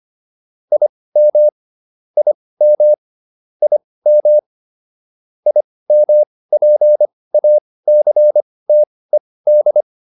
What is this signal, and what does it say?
Morse code, heard as a keyed tone, about 25 wpm, spaced as for 18 wpm, 600 Hz. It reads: IM IM IM IMPACTED